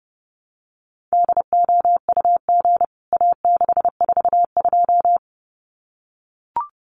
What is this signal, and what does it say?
Morse code, heard as a keyed tone, about 30 wpm, 700 Hz. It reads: DOUG A642
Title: CWT Exchange 30wpm